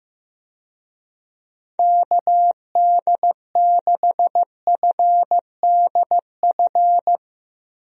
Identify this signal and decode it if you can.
Morse code, heard as a keyed tone, about 15 wpm, 700 Hz. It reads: KD6FDF
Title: USA Call Signs - 6 Characters - 15wpm